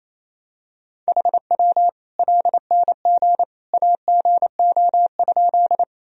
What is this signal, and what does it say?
Morse code, heard as a keyed tone, about 28 wpm, 700 Hz. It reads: HW LNG AGO?
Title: Sentences from Top 500 Words and Abbreviations 28wpm